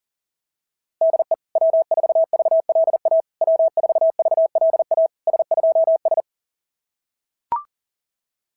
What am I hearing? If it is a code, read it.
Morse code, heard as a keyed tone, about 40 wpm, 660 Hz. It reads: DE W4VLA W4VLA S2S
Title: SOTA QSO Practice 40wpm